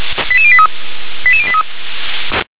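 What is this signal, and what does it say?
Five-tone selective call (ZVEI), heard as quick runs of five tones, tones of about 70 ms. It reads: 88083 88083
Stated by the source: Five-tone ZVEI selective call received on an SDR.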